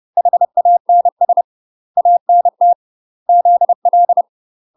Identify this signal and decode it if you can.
Morse code, heard as a keyed tone, about 30 wpm, 700 Hz. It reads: HANS ANT ZL
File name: UK MASTER QSOs TRAINING / 30WPM / UK MASTER QSOs 0019